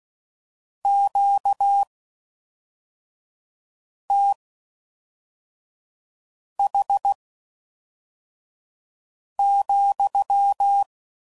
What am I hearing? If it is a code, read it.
Morse code, heard as a keyed tone, about 16 wpm, spaced as for 4 wpm, 800 Hz. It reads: QTH,